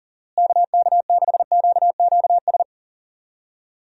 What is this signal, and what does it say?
Morse code, heard as a keyed tone, about 40 wpm, 700 Hz. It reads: KK6QQS
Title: USA Call Signs - 6 Characters - Repeated 2x - 40wpm